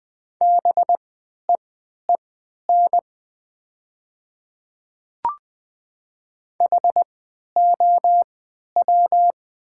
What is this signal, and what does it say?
Morse code, heard as a keyed tone, about 20 wpm, spaced as for 11 wpm, 690 Hz. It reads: BEEN HOW
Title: 3x Character Spacing - Top 100 Words - 20wpm